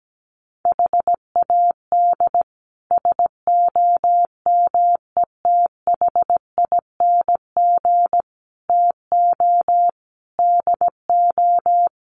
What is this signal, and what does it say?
Morse code, heard as a keyed tone, about 17 wpm, 700 Hz. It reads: HAD SOMETHING TO DO
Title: Sets of 5 Words 17wpm